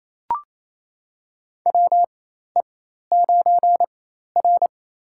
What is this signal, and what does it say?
Morse code, heard as a keyed tone, about 28 wpm, spaced as for 13 wpm, 700 Hz. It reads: WE9R
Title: USA Call Signs - 4 Characters - 4x Character Spacing - 28wpm